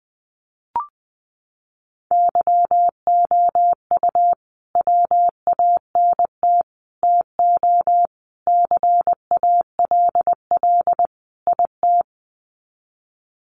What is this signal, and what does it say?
Morse code, heard as a keyed tone, about 20 wpm, 700 Hz. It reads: YOU WANT TO CALL IT
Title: Sets of 5 Words Encore2 20wpm